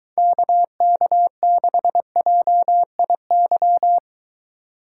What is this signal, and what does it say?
Morse code, heard as a keyed tone, about 21 wpm, 700 Hz. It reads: KK6JIY